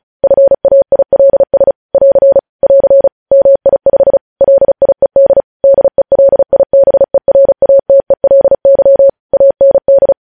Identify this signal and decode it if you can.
Morse code, heard as a keyed tone, about 36 wpm, 550 Hz. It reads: FAILS <AR> <AR> MI5 LIED DELIBERATELY AND
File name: podcast / News headlines-1x-35x35